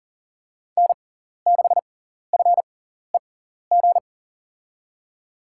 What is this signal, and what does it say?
Morse code, heard as a keyed tone, about 40 wpm, spaced as for 14 wpm, 700 Hz. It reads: N6FEG
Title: USA Call Signs - 5 Characters - 6x Character Spacing - 40wpm